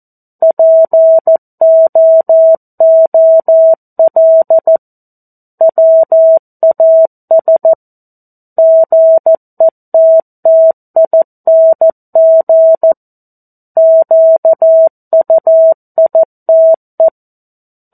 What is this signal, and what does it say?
Morse code, heard as a keyed tone, about 14 wpm, 650 Hz. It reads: POOL WAS GETTING QUITE
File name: 02-Chapter-02 14 WPM